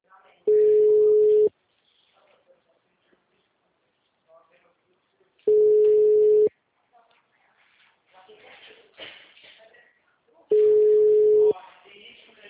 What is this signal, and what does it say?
Ringback tone (Europe, 425 Hz), with speech.